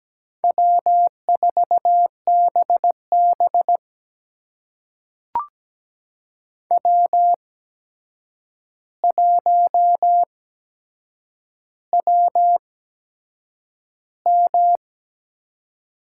Morse code, heard as a keyed tone, about 17 wpm, spaced as for 7 wpm, 700 Hz. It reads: W4BB W1WM
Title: USA Call Signs - 8x Character Spacing - 17wpm